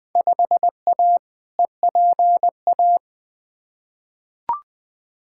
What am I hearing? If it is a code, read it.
Morse code, heard as a keyed tone, about 20 wpm, 700 Hz. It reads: 5A EPA
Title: Field Day Exchanges - 20wpm